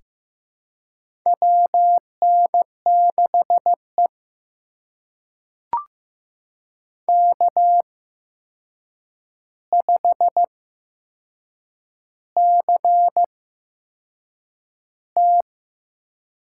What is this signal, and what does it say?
Morse code, heard as a keyed tone, about 15 wpm, spaced as for 6 wpm, 700 Hz. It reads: WN6E K5CT
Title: USA Call Signs - 4 Characters - 8x Character Spacing - 15wpm